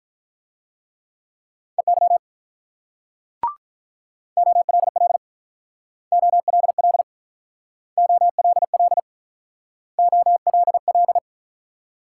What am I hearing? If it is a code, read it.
Morse code, heard as a keyed tone, about 46 wpm, 700 Hz. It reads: EY OLL OLL OLL OLL